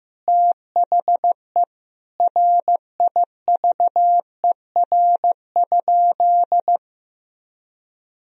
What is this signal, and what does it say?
Morse code, heard as a keyed tone, about 14 wpm, 700 Hz. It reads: THE RIVER?
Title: Sentences from Top 1000 Words - Review Part 4 - 15wpm